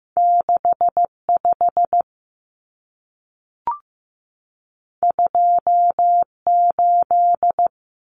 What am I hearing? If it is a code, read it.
Morse code, heard as a keyed tone, about 15 wpm, 700 Hz. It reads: 65 28
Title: Numbers - 2-Digit 15wpm